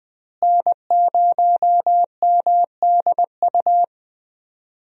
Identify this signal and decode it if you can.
Morse code, heard as a keyed tone, about 20 wpm, 700 Hz. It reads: N0MDU
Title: USA Call Signs - 5 Characters - Repeated 3x - 20wpm